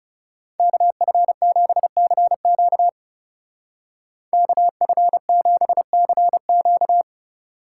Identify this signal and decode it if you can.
Morse code, heard as a keyed tone, about 37 wpm, 700 Hz. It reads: KF7CQ KF7CQ
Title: Speed Racing - USA Call Signs 30wpm